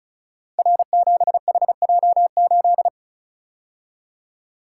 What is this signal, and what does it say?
Morse code, heard as a keyed tone, about 36 wpm, 700 Hz. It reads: R7HJ8
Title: Code Groups - 5 Letter:Numbers - 35wpm